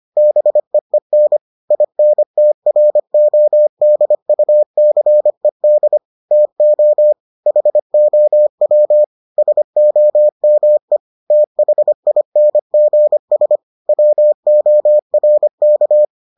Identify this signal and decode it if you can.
Morse code, heard as a keyed tone, about 25 wpm, 600 Hz. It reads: BEEN INTRODUCED TO HOW SOME THINGS WORK